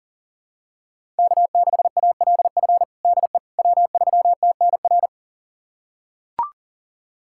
Morse code, heard as a keyed tone, about 40 wpm, 700 Hz. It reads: K6ALF DE W3TNR
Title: USA Callsign DE Callsign - 5 Characters - 40wpm